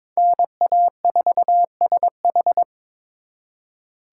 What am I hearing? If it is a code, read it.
Morse code, heard as a keyed tone, about 26 wpm, 700 Hz. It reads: NA4SH
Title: Speed Racing - USA 2x2 Call Signs 22wpm